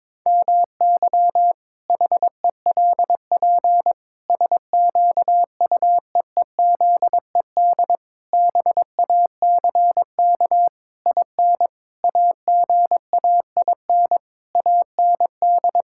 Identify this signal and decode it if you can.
Morse code, heard as a keyed tone, about 22 wpm, 700 Hz. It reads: MY HELP SQUEEZED BACK IN AGAIN AND
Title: Pangram Challenge v2 - 22wpm